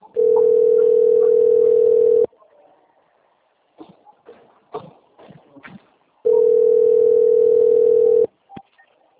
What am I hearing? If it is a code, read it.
Ringback tone (North America), 440 + 480 Hz.